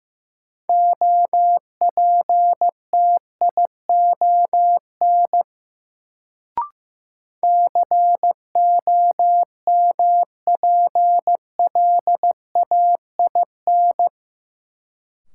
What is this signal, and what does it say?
Morse code, heard as a keyed tone, about 15 wpm, 700 Hz. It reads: OPTION COMPLAIN
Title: Sentences from Top 1300 Words - New Word Drill Alt - 15wpm